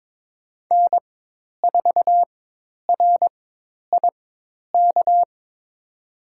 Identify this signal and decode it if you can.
Morse code, heard as a keyed tone, about 22 wpm, spaced as for 10 wpm, 700 Hz. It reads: N4RIK
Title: USA Call Signs - 5 Characters - 4x Character Spacing - 22wpm